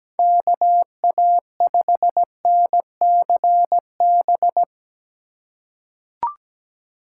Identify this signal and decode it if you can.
Morse code, heard as a keyed tone, about 17 wpm, 700 Hz. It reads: KA5NCB
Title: USA Call Signs - 6 Characters - 8x Character Spacing - 17wpm